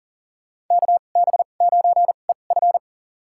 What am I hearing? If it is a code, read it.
Morse code, heard as a keyed tone, about 40 wpm, spaced as for 29 wpm, 700 Hz. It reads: KB9EF
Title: USA Call Signs - 5 Characters - 2x Character Spacing - 40wpm